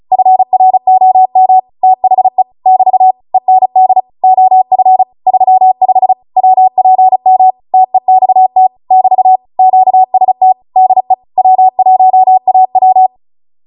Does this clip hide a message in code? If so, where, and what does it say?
Yes: Morse code from 0.1 to 13.1 s, heard as a keyed tone, about 36 wpm, 750 Hz. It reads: FROM THE <BT> END OF 35 WPM TEXT <BT> QST DE W1AW